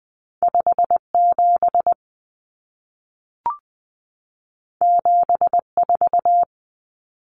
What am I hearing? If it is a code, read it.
Morse code, heard as a keyed tone, about 20 wpm, 700 Hz. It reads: 57 74
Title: Common Cut Numbers - 2-Digit 20wpm